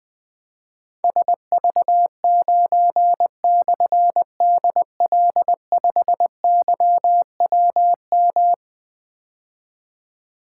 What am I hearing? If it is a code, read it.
Morse code, heard as a keyed tone, about 20 wpm, 700 Hz. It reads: SV9/DL5YWM
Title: Difficult Call Signs 20wpm